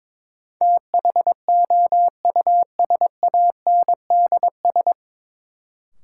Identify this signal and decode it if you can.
Morse code, heard as a keyed tone, about 22 wpm, 700 Hz. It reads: THOUSANDS
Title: Sets of 4 Words - New Word Drill - 22wpm